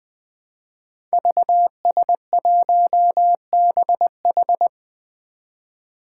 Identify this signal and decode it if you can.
Morse code, heard as a keyed tone, about 20 wpm, 700 Hz. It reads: VS1BH
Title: Five Challenging Letter-Number Combinations 20wpm